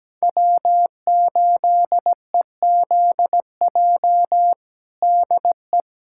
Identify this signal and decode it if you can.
Morse code, heard as a keyed tone, about 17 wpm, 700 Hz. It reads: W8EZJ DE
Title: USA Callsign DE Callsign 17wpm